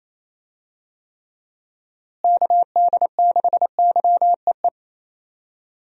Morse code, heard as a keyed tone, about 27 wpm, 700 Hz. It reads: KD6YEE
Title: USA Call Signs - 6 Characters - Repeated 3x - 28wpm